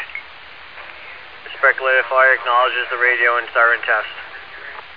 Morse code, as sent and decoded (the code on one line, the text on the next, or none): none